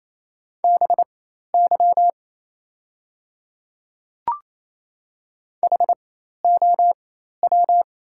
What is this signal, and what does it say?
Morse code, heard as a keyed tone, about 28 wpm, spaced as for 13 wpm, 700 Hz. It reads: BY HOW
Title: 4x Character Spacing - Top 100 Words - 28wpm